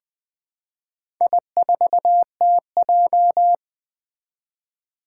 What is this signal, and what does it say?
Morse code, heard as a keyed tone, about 20 wpm, 700 Hz. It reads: I4TJ
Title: Four Challenging Letter-Number Combinations 20wpm